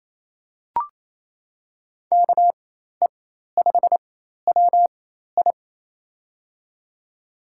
Morse code, heard as a keyed tone, about 27 wpm, spaced as for 13 wpm, 700 Hz. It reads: KE5WI
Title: USA Call Signs - 5 Characters - 4x Character Spacing - 28wpm